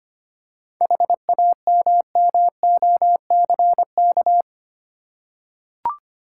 Morse code, heard as a keyed tone, about 25 wpm, 690 Hz. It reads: HAMMOCK
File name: practice / 25wpm / Easy Words with Hints - 25wpm